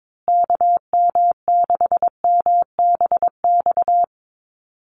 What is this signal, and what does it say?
Morse code, heard as a keyed tone, about 22 wpm, 700 Hz. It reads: KM6MBX
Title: USA Call Signs Repeated 3x 22wpm